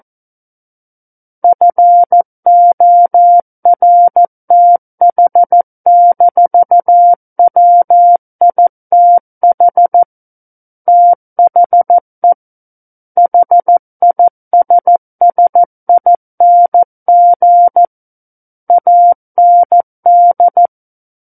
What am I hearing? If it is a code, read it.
Morse code, heard as a keyed tone, about 14 wpm, 700 Hz. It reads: FORTH-WITH THE HISSING AND
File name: Book One / 05-Chapter 5 - 14 WPM